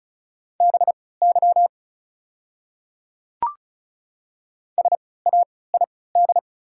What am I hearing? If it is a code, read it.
Morse code, heard as a keyed tone, about 34 wpm, spaced as for 20 wpm, 690 Hz. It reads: BY SAID
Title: 3x Character Spacing - Top 100 Words - 35wpm